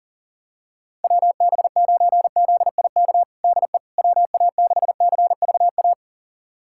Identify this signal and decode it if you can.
Morse code, heard as a keyed tone, about 40 wpm, 700 Hz. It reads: WB9ZIK DE WA6CVA
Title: USA Callsign DE Callsign - 6 Characters - 40wpm